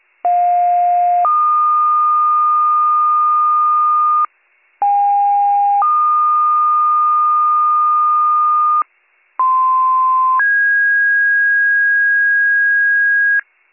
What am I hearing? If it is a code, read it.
Two-tone page, heard as a one-second tone, then a three-second tone, about 710 then 1220 Hz, about 810 then 1220 Hz, about 1040 then 1650 Hz. It is heard from 0.2 to 13.4 s.